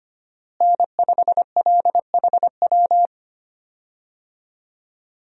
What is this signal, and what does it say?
Morse code, heard as a keyed tone, about 25 wpm, 700 Hz. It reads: N5LHW
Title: USA Call Signs - 5 Characters - Repeated 3x - 25wpm